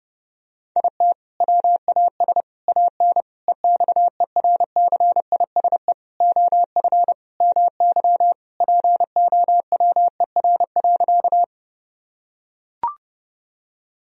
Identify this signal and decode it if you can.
Morse code, heard as a keyed tone, about 32 wpm, 700 Hz. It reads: IT WAS AN EXERCISE OF MY POWER.
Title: Sentences from Top 1000 Words - Review Part 4 - 30wpm